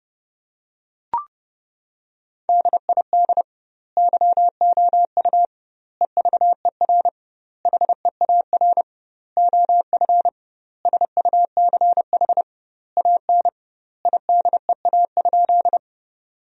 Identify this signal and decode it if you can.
Morse code, heard as a keyed tone, about 30 wpm, 700 Hz. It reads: DID YOU EVER HEAR OF SUCH AN IDEA?
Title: 2x Word-Spacing - Sentences from Top 400 Words - 30wpm